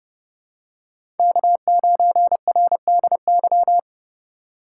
Morse code, heard as a keyed tone, about 30 wpm, 700 Hz. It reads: K9RDY
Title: USA Call Signs - 5 Characters - 2x Character Spacing - 30wpm